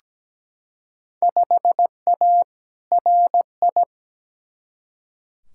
Morse code, heard as a keyed tone, about 17 wpm, 700 Hz. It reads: 5A RI
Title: Field Day Exchanges Encore - 17wpm